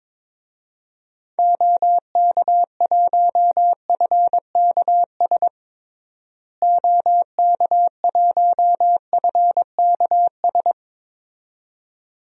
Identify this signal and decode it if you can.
Morse code, heard as a keyed tone, about 22 wpm, 700 Hz. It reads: OK1FKS OK1FKS
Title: DXCC-100 Call Signs - 6 Characters - Repeated 2x - 22wpm